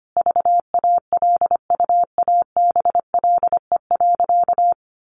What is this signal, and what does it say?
Morse code, heard as a keyed tone, about 25 wpm, 700 Hz. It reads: VALUABLE.
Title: Sentences from Top 1100 Words 25wpm